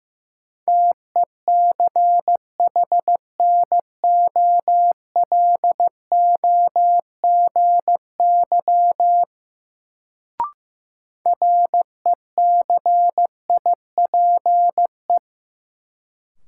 Morse code, heard as a keyed tone, about 15 wpm, 700 Hz. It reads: TECHNOLOGY RECIPE
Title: Sentences from Top 1200 Words - New Word Drill Alt - 15wpm